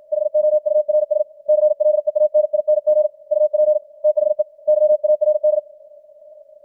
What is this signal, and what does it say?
Morse code, heard as a keyed tone, about 56 wpm, 610 Hz. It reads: HOURS CLEANING UP THE YARD
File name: archived / CWBros